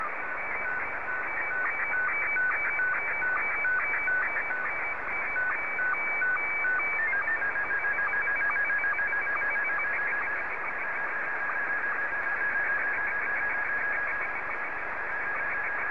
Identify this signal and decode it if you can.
SSTV (slow-scan television), Scottie 1.